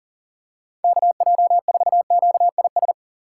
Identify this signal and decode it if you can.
Morse code, heard as a keyed tone, about 37 wpm, 700 Hz. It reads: KJ4QIS
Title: USA Call Signs - 6 Characters - 3x Character Spacing - 40wpm